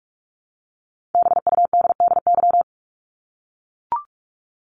Morse code, heard as a keyed tone, about 44 wpm, 700 Hz. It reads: BUDDY
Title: Top 1751-2000 Words 45wpm